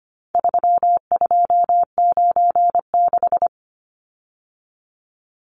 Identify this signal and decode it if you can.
Morse code, heard as a keyed tone, about 25 wpm, 700 Hz. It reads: 3296
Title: Numbers - 4-Digit 25wpm